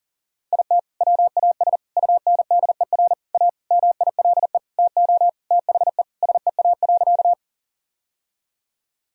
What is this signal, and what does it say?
Morse code, heard as a keyed tone, about 40 wpm, 700 Hz. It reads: IT WAS UNDER A MILE TO THE SEA.
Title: Sentences from Top 1000 Words - Review Part 1 - 40wpm